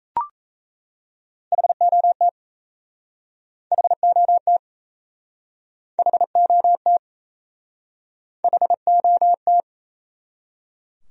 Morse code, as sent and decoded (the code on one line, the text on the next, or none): .... --- - / .... --- - / .... --- - / .... --- -
HOT HOT HOT HOT